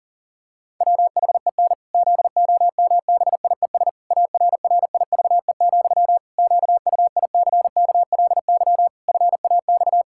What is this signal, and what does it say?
Morse code, heard as a keyed tone, about 40 wpm, 700 Hz. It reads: WHEN ZOMBIES ARRIVE, QUICKLY FAX
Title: Pangram Challenge v2 - 40wpm